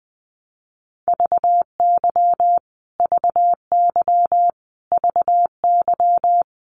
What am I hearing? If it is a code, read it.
Morse code, heard as a keyed tone, about 20 wpm, 700 Hz. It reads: VY VY VY
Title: Abbreviations Repeated 3x - 20wpm